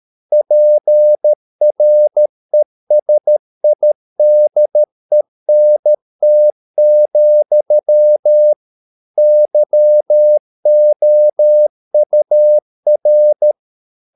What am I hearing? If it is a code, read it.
Morse code, heard as a keyed tone, about 13 wpm, 600 Hz. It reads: PRESIDENT, YOUR